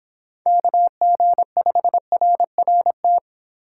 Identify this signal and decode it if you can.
Morse code, heard as a keyed tone, about 26 wpm, 700 Hz. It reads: KG5RRT